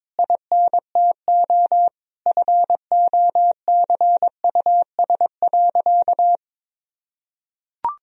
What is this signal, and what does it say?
Morse code, heard as a keyed tone, about 22 wpm, 700 Hz. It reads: INTO FOCUS.
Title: Sentences from Top 1000 Words - Review Part 4 - 22wpm